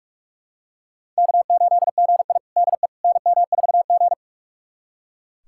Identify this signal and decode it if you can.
Morse code, heard as a keyed tone, about 41 wpm, 700 Hz. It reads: K8GI DE NM4G
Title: USA Callsign DE Callsign - 4 Characters - 45wpm